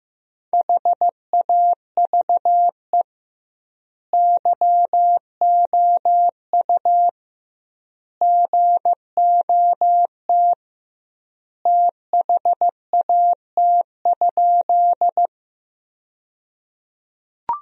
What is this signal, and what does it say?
Morse code, heard as a keyed tone, about 16 wpm, 700 Hz. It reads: HAVE YOU GOT THAT?
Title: 2x Word-Spacing - Sentences from Top 300 Words - 15wpm